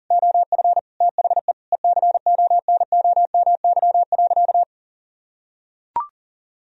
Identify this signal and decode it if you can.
Morse code, heard as a keyed tone, about 40 wpm, 700 Hz. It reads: OF THE ECONOMY.
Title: Sentences from Top 1000 Words - Review Part 3 - 40wpm